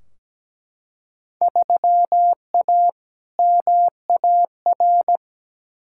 Morse code, heard as a keyed tone, about 17 wpm, 700 Hz. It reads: 3A MAR
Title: Field Day Exchanges Encore - 17wpm